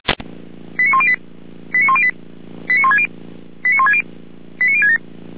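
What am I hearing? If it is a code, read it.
Five-tone selective call (ZVEI), heard as quick runs of five tones, tones of about 70 ms. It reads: 89108 89108 89160 89160 89067
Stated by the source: ZVEI five-tone selective call received with an SDR.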